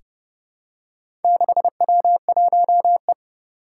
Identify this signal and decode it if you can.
Morse code, heard as a keyed tone, about 30 wpm, 700 Hz. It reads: 6W1E
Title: Four Challenging Letter-Number Combinations 30wpm